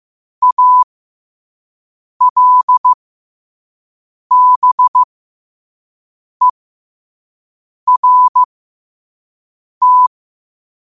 Morse code, heard as a keyed tone, about 15 wpm, spaced as for 5 wpm, 1000 Hz. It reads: ALBERT
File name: qotdfiles / 5wpm